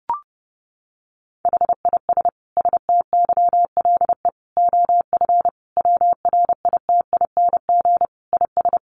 Morse code, heard as a keyed tone, about 30 wpm, 700 Hz. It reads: HIS STYLE OF WRITING IS
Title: Sentences from Top 800 Words 30wpm